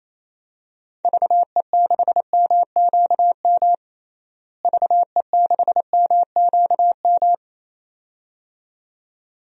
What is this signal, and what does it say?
Morse code, heard as a keyed tone, about 28 wpm, 700 Hz. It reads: VE6MQM VE6MQM
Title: DXCC-100 Call Signs - 6 Characters - Repeated 2x - 28wpm